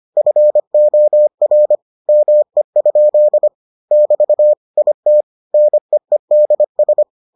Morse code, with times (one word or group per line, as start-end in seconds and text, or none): 0.2-1.8 FOR
2.1-3.5 ME?
3.9-4.5 <BT>
4.8-5.2 IT
5.5-7.0 NEEDS